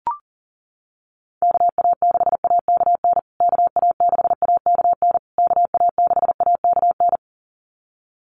Morse code, heard as a keyed tone, about 40 wpm, 700 Hz. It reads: KA6AKN KA6AKN KA6AKN
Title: USA Call Signs Repeated 3x 40wpm